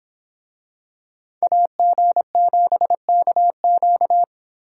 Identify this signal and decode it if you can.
Morse code, heard as a keyed tone, about 26 wpm, 700 Hz. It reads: AG7KQ